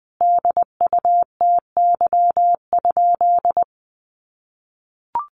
Morse code, heard as a keyed tone, about 20 wpm, 700 Hz. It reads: DUTY?
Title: Sentences from Top 1500 Words 20wpm